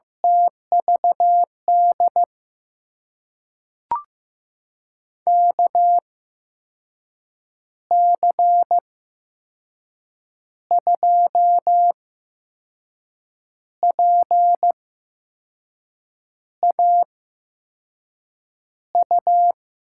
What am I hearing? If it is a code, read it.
Morse code, heard as a keyed tone, about 15 wpm, spaced as for 5 wpm, 700 Hz. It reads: TVD KC2PAU
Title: USA Call Signs - 6 Characters - 8x Character Spacing - 15wpm